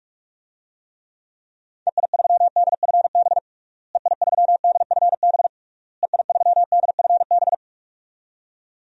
Morse code, heard as a keyed tone, about 46 wpm, 700 Hz. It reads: EI3DFB EI3DFB EI3DFB
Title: DXCC-100 Call Signs - 6 Characters - Repeated 3x - 45wpm